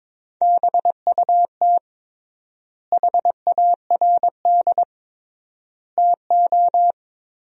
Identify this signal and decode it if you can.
Morse code, heard as a keyed tone, about 22 wpm, 700 Hz. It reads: BUT HARD TO
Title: 3x Word-Spacing - Sentences from Top 300 Words - 22wpm